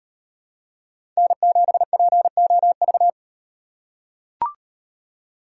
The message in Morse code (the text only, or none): N7POV